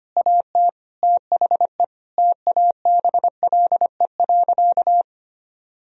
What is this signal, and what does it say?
Morse code, heard as a keyed tone, about 25 wpm, 700 Hz. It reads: AT THE TABLE.
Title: Sentences from Top 1000 Words - Review Part 2 - 25wpm